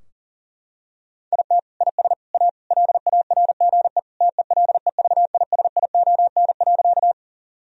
Morse code, heard as a keyed tone, about 40 wpm, 700 Hz. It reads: IT IS A LARGE TELEVISION.
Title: Sentences from Top 1000 Words - Review Part 4 - 40wpm